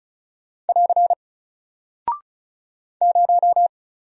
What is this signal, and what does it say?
Morse code, heard as a keyed tone, about 36 wpm, 700 Hz. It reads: <AR> 0
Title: Single Character (Letter:Number:Punctuation:Prosign) Alt 35wpm